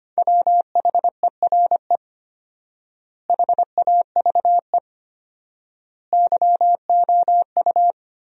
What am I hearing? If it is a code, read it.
Morse code, heard as a keyed tone, about 25 wpm, 700 Hz. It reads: WHERE HAVE YOU
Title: 4x Word-Spacing - Sentences from Top 300 Words - 25wpm